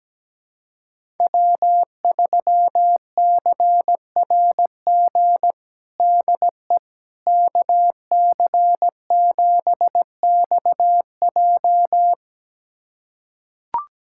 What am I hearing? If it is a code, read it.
Morse code, heard as a keyed tone, about 17 wpm, 700 Hz. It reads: W3CRG DE KC7XJ
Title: USA Callsign DE Callsign - 5 Characters - 17wpm